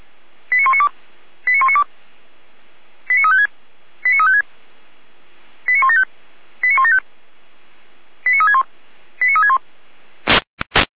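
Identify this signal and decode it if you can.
Five-tone selective call (ZVEI), heard as quick runs of five tones, tones of about 70 ms. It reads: 89192 89192 89356 89356 89175 89175 89361 89361
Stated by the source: ZVEI five-tone call 89500 received by SDR, fire brigade Feuerwehr Orlamünde.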